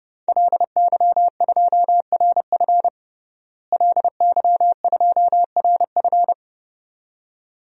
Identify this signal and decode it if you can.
Morse code, heard as a keyed tone, about 30 wpm, 700 Hz. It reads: LY2RF LY2RF
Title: DXCC-100 Call Signs - 5 Characters - Repeated 2x - 30wpm